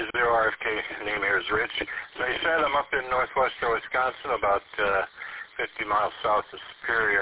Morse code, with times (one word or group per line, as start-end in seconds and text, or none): none